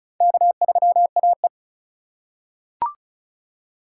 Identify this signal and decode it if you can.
Morse code, heard as a keyed tone, about 36 wpm, 700 Hz. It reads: K3AE
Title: USA Call Signs - 4 Characters - 4x Character Spacing - 35wpm